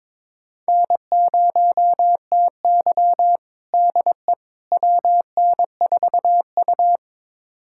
Morse code, heard as a keyed tone, about 22 wpm, 700 Hz. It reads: N0TY DE WN4U
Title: USA Callsign DE Callsign - 4 Characters - 22wpm